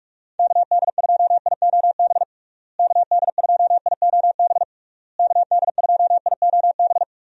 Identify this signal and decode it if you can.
Morse code, heard as a keyed tone, about 41 wpm, 700 Hz. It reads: KD2IOB KD2IOB KD2IOB
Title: USA Call Signs - 6 Characters - Repeated 3x - 45wpm